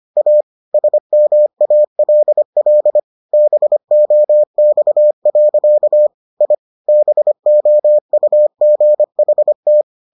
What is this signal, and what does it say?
Morse code, heard as a keyed tone, about 25 wpm, 600 Hz. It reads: A SMALL BOX. I BOUGHT